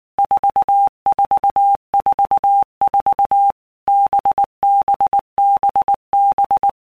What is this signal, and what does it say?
Morse code, heard as a keyed tone, about 19 wpm, 800 Hz. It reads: 4444 BBBB
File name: Basic Letter Groups By Blocks / Groups@18 / VB46@18